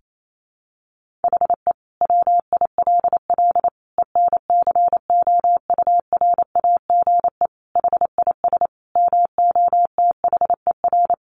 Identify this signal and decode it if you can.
Morse code, heard as a keyed tone, about 28 wpm, 700 Hz. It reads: HE WILL ENCOURAGE HIS MOTHER
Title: Sentences from Top 1400 Words 28wpm